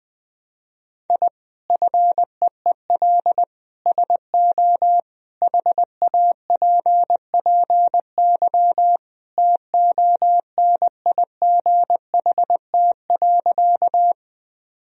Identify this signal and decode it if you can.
Morse code, heard as a keyed tone, about 21 wpm, 700 Hz. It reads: I FEEL SO HAPPY TONIGHT.
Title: Sentences from Top 1000 Words - Review Part 3 - 20wpm